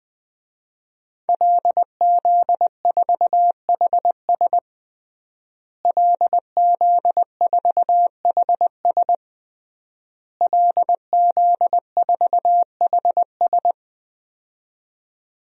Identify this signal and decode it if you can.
Morse code, heard as a keyed tone, about 20 wpm, 700 Hz. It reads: LZ4HS LZ4HS LZ4HS
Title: DXCC-100 Call Signs - 5 Characters - Repeated 3x - 20wpm